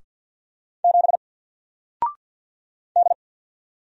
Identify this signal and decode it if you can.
Morse code, heard as a keyed tone, about 50 wpm, 700 Hz. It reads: 7 D
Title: Single Letter-Number Alt 50wpm